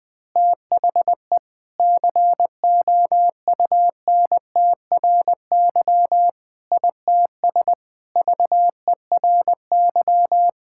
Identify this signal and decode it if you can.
Morse code, heard as a keyed tone, about 20 wpm, 700 Hz. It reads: THE COUNTRY ITS VERY
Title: Sentences with 5-Word Phrases - Spoken Beginning - 20wpm